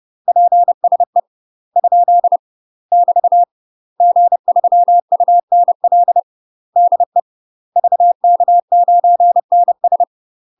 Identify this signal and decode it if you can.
Morse code, heard as a keyed tone, about 30 wpm, 700 Hz. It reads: PSE ? <BT> G3UNL DE VK9NS
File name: UK MASTER QSOs TRAINING / 30WPM / UK MASTER QSOs 0023